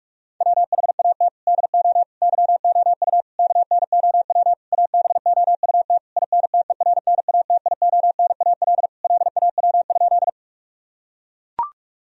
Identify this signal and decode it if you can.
Morse code, heard as a keyed tone, about 41 wpm, 700 Hz. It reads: WHAT DO YOU KNOW ABOUT INTERNATIONAL LAW?
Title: Sentences from Top 1000 Words - Review Part 4 - 45wpm